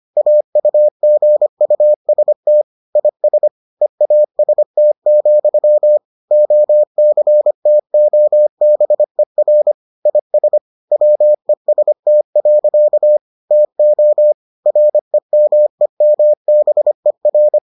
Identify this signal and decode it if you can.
Morse code, heard as a keyed tone, about 25 wpm, 600 Hz. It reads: AUGUST IS EAST, OCTOBER IS WEST. TO REMEMBER